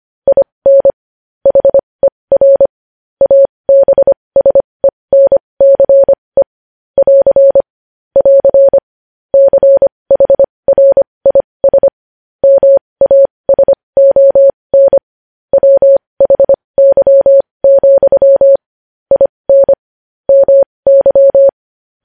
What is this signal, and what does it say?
Morse code, heard as a keyed tone, about 25 wpm, spaced as for 20 wpm, 550 Hz. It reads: IN HER ABSENCE <AR> <AR> CHRIS MASON WHY, IN MY
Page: News Headlines in Morse Code at 20 WPM